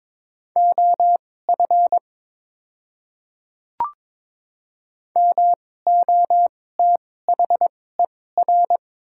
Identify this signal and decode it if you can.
Morse code, heard as a keyed tone, about 22 wpm, spaced as for 16 wpm, 700 Hz. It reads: OF MOTHER
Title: 2x Character Spacing - Top 200 Words - 22wpm